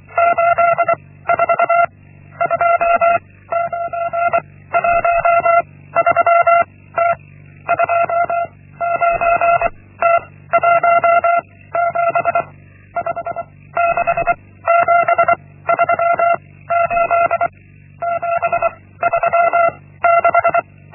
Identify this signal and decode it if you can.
Morse code, heard as a keyed tone, about 24 wpm, 680 Hz. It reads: 8 4 2 9 1 3 T 2 9 T 1 7 5 6 7 3 8 7 3 6
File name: Linksmakalnis-1